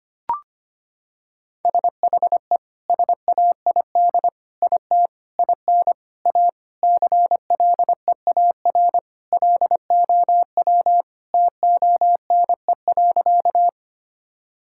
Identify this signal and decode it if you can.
Morse code, heard as a keyed tone, about 25 wpm, 700 Hz. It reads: SHE SAID IT IN A CLEAR LOW TONE.
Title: Sentences from Top 1600 Words 25wpm